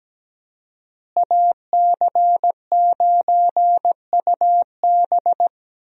Morse code, heard as a keyed tone, about 17 wpm, 700 Hz. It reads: AC9UB
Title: USA Call Signs - 5 Characters - Repeated 2x - 17wpm